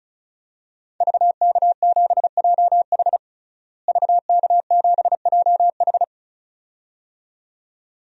Morse code, heard as a keyed tone, about 36 wpm, 700 Hz. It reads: VK7JH VK7JH
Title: DXCC-100 Call Signs - 5 Characters - Repeated 2x - 35wpm